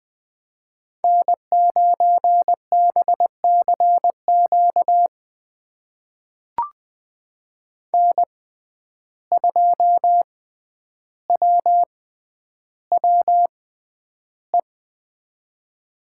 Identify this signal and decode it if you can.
Morse code, heard as a keyed tone, about 20 wpm, spaced as for 10 wpm, 700 Hz. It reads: N9BCQ N2WWE
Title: USA Call Signs - 5 Characters - 6x Character Spacing - 20wpm